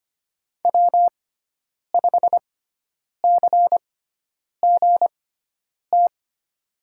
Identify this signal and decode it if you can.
Morse code, heard as a keyed tone, about 25 wpm, spaced as for 9 wpm, 700 Hz. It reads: W5CGT
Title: USA Call Signs - 5 Characters - 6x Character Spacing - 25wpm